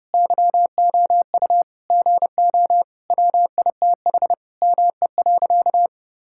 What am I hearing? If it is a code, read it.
Morse code, heard as a keyed tone, about 30 wpm, 700 Hz. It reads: YOU GO WITH ME.
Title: Sentences from Top 1000 Words - Review Part 6 - 30wpm